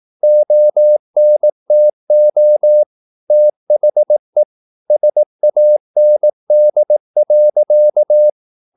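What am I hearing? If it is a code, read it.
Morse code, heard as a keyed tone, about 18 wpm, 600 Hz. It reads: ONTO THE SAND.
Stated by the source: Morse practice at 18 wpm, English text, from CW med Gnister.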